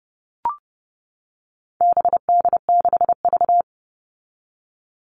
Morse code, heard as a keyed tone, about 30 wpm, 700 Hz. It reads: BD6V
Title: All Cut Numbers - 4-Digit 30wpm